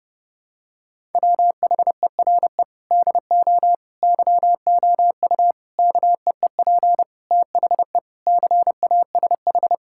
Morse code, heard as a keyed tone, about 30 wpm, 700 Hz. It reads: WHERE DO YOU KEEP THE CASH
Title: Sentences from Top 1000 Words - Review Part 5 - 30wpm